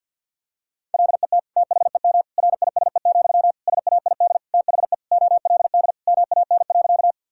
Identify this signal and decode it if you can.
Morse code, heard as a keyed tone, about 50 wpm, 700 Hz. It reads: LET THEM RISE, SAID THE OLD MAN.
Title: Sentences from Top 1000 Words - Review Part 6 - 50wpm